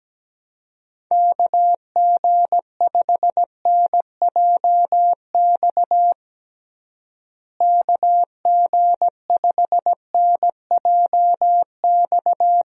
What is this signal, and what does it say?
Morse code, heard as a keyed tone, about 17 wpm, 700 Hz. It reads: KG5NJX KG5NJX
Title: USA Call Signs - 6 Characters - Repeated 2x - 17wpm